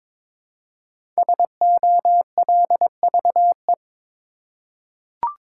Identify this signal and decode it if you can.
Morse code, heard as a keyed tone, about 22 wpm, 700 Hz. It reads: SOLVE
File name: 5 Letter Words - First 3 Letters Given - 22wpm